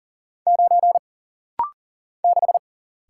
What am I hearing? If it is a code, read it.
Morse code, heard as a keyed tone, about 40 wpm, 700 Hz. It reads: 9 6
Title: Single Letter-Number - Rapid-Fire 40wpm